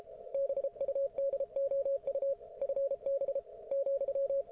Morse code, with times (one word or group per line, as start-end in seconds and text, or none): 0.3-2.3 BUDOU
2.6-3.4 FB
3.7-4.4 ,